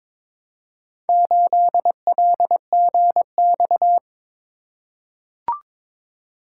Morse code, heard as a keyed tone, about 22 wpm, 700 Hz. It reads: 8LGX
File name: practice / 22wpm / Four Letter-Number Combinations - 22wpm